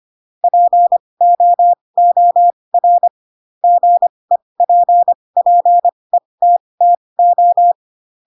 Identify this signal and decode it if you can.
Morse code, heard as a keyed tone, about 25 wpm, spaced as for 20 wpm, 700 Hz. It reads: POOR GEPPETTO
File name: Pinocchio / Pinocchio ch3 15 25 ik1hge